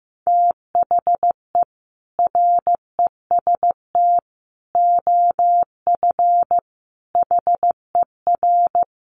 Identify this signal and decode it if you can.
Morse code, heard as a keyed tone, about 15 wpm, 700 Hz. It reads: THE REST OF HER
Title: Sets of 5 Words Encore 15wpm